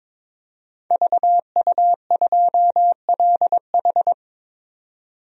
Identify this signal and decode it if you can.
Morse code, heard as a keyed tone, about 22 wpm, 700 Hz. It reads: VU2LH
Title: DXCC-100 Call Signs - 5 Characters - 22wpm